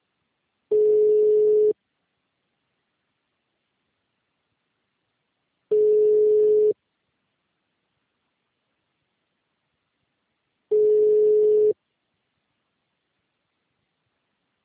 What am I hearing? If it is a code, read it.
Ringback tone (Europe, 425 Hz).
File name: out- -2004- - - .28497